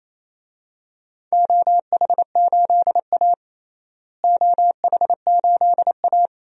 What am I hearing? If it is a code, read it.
Morse code, heard as a keyed tone, about 28 wpm, 700 Hz. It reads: OH8A OH8A
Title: DXCC-100 Call Signs - 4 Characters - Repeated 2x - 28wpm